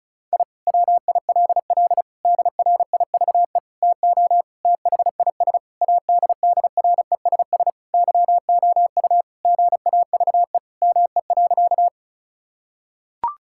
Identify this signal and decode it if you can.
Morse code, heard as a keyed tone, about 34 wpm, 700 Hz. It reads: I WILL DRIVE TO THIS ADDRESS YOU GAVE ME.
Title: Sentences from Top 1000 Words - Review Part 6 - 35wpm